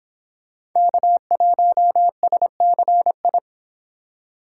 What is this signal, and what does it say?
Morse code, heard as a keyed tone, about 26 wpm, 700 Hz. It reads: K1SCI